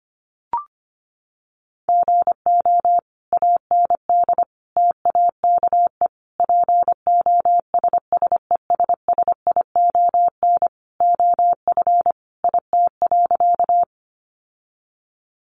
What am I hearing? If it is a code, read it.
Morse code, heard as a keyed tone, about 25 wpm, 700 Hz. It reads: GO AND TAKE POSSESSION OF IT.
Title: Sentences from Top 1400 Words 25wpm